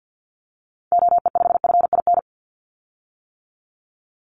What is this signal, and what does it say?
Morse code, heard as a keyed tone, about 50 wpm, 700 Hz. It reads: OE5FIN
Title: Call Signs - 50wpm